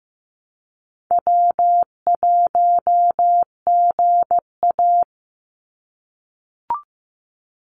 Morse code, heard as a keyed tone, about 15 wpm, 700 Hz. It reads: W1GA
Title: USA Call Signs 15wpm